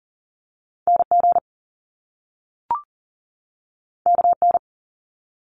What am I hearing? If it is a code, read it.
Morse code, heard as a keyed tone, about 38 wpm, 700 Hz. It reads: NG KN
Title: Top Two Letter Combinations in Words 40wpm